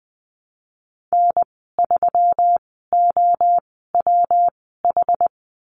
Morse code, heard as a keyed tone, about 20 wpm, spaced as for 15 wpm, 700 Hz. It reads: N3OWH
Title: 179 Lesson 29g - Slash - Call Signs 20wpm